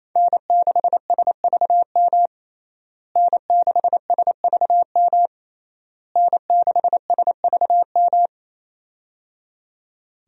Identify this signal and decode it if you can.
Morse code, heard as a keyed tone, about 28 wpm, 700 Hz. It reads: N6SVM N6SVM N6SVM
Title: USA Call Signs - 5 Characters - Repeated 3x - 28wpm